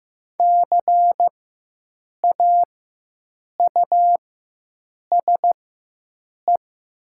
Morse code, heard as a keyed tone, about 15 wpm, spaced as for 7 wpm, 700 Hz. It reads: CAUSE
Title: 4x Character Spacing - Top 200 Words - 15wpm